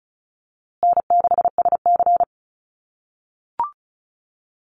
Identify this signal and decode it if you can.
Morse code, heard as a keyed tone, about 34 wpm, 700 Hz. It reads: N6SC
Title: USA Call Signs Repeated 3x 35wpm